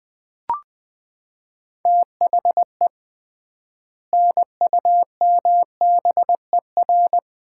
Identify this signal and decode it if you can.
Morse code, heard as a keyed tone, about 20 wpm, 700 Hz. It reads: THE NUMBER
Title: 3x Word-Spacing - Sets of 3 Words - 20wpm